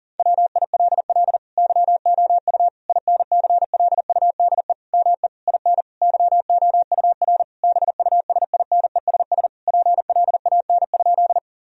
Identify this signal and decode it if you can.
Morse code, heard as a keyed tone, about 40 wpm, 700 Hz. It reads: WILL YOU INCLUDE ME IN YOUR BUSINESS PLAN?
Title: Sentences from Top 1000 Words - Review Part 3 - 40wpm